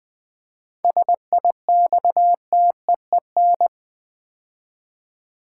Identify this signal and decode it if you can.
Morse code, heard as a keyed tone, about 20 wpm, 700 Hz. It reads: SIXTEEN
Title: 7 Letter Words - First 3 Letters Given - 20wpm